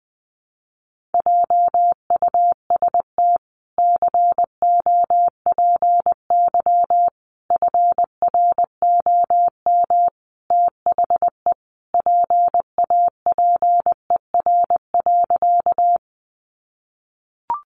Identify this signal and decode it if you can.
Morse code, heard as a keyed tone, about 20 wpm, 700 Hz. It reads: JUST COPY FROM THE PAPER.
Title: Sentences from Top 1400 Words 20wpm